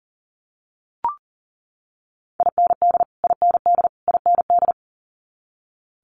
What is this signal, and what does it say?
Morse code, heard as a keyed tone, about 40 wpm, 700 Hz. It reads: IND IND IND
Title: Multi-letter Phonemes Repeated 3x 40wpm